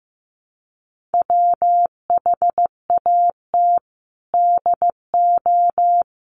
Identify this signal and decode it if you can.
Morse code, heard as a keyed tone, about 15 wpm, 700 Hz. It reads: WHAT DO
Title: Sentences from Top 100 Words 15wpm